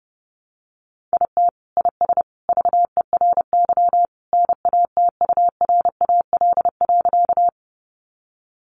Morse code, heard as a keyed tone, about 30 wpm, 700 Hz. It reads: IT IS VERY NATURAL.
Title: Sentences from Top 700 Words 30wpm